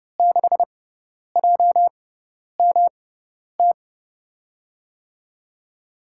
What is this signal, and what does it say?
Morse code, heard as a keyed tone, about 30 wpm, spaced as for 10 wpm, 700 Hz. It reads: 6JMT
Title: USA Call Signs - 6 Characters - 6x Character Spacing - 30wpm